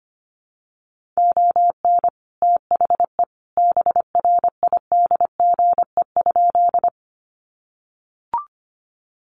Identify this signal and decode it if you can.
Morse code, heard as a keyed tone, about 25 wpm, 700 Hz. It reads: ON THE BRIDGE?
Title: Sentences from Top 1200 Words 25wpm